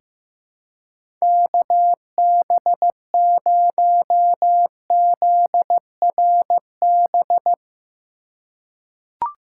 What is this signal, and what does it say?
Morse code, heard as a keyed tone, about 15 wpm, 700 Hz. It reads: KB0ZRB
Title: USA Call Signs - 6 Characters - 3x Character Spacing - 15wpm